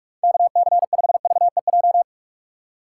Morse code, heard as a keyed tone, about 47 wpm, 700 Hz. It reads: KC5VEJ
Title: USA Call Signs - 6 Characters - 4x Character Spacing - 45wpm